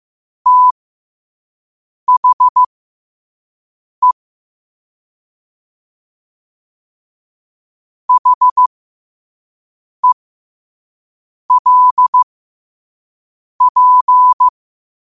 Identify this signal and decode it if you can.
Morse code, heard as a keyed tone, about 15 wpm, spaced as for 5 wpm, 1000 Hz. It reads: THE HELP
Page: New quotes every day in morse code at 5 Words per minute.